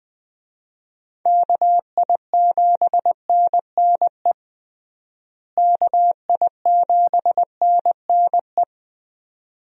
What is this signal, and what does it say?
Morse code, heard as a keyed tone, about 20 wpm, 700 Hz. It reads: KI7NNE KI7NNE
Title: USA Call Signs - 6 Characters - Repeated 2x - 20wpm